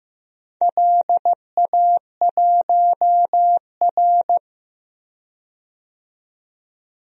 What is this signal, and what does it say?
Morse code, heard as a keyed tone, about 15 wpm, 700 Hz. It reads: LA1R
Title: DXCC-100 Call Signs - 4 Characters - 15wpm